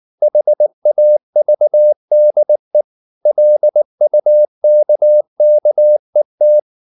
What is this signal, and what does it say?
Morse code, heard as a keyed tone, about 19 wpm, 600 Hz. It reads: HAVDE LUKKET
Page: Ferie 19 wpm